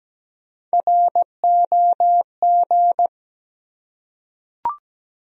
Morse code, heard as a keyed tone, about 17 wpm, 700 Hz. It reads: ROG
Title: USA Common Names - Max 3 Letters - 17wpm